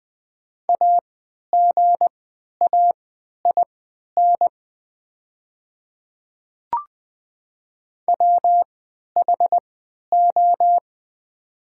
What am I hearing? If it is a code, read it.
Morse code, heard as a keyed tone, about 20 wpm, spaced as for 11 wpm, 690 Hz. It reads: AGAIN WHO
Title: 3x Character Spacing - Top 300 Words - 20wpm